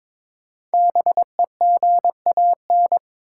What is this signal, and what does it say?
Morse code, heard as a keyed tone, about 22 wpm, 700 Hz. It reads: BEGAN
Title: Sentences from Top 300 Words - New Word Drill - 22wpm